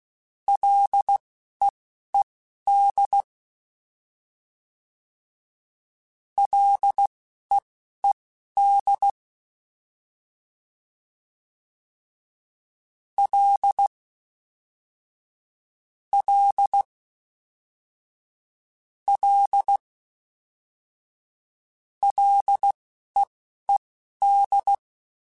Morse code, heard as a keyed tone, about 16 wpm, spaced as for 11 wpm, 800 Hz. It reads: LEED LEED L L L LEED
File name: Ls / Single L m3s / Single L Prefixe 3x2 Twice @10